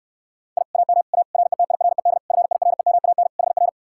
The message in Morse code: I AM A RETIRED FIREMAN HR

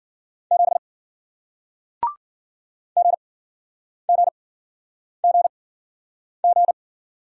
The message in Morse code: 7 GGGG

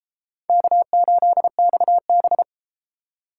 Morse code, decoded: K8XB

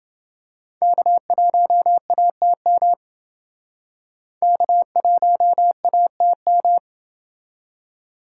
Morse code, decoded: K1ATM K1ATM

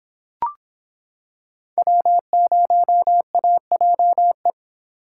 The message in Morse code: W0AJE

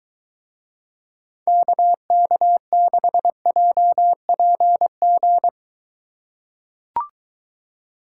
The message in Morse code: KK6JPG

